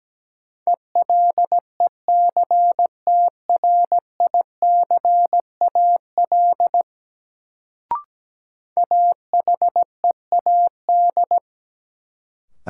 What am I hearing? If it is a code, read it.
Morse code, heard as a keyed tone, about 17 wpm, 700 Hz. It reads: ELECTRICAL AHEAD
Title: Sentences from Top 1000 Words - Review Part 6 - New Word Drill - 17wpm